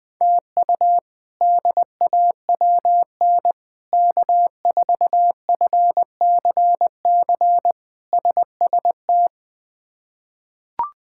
Morse code, heard as a keyed tone, about 20 wpm, 700 Hz. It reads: TU DAWN K4FCC SST
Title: K1USN SST QSO Practice - 20wpm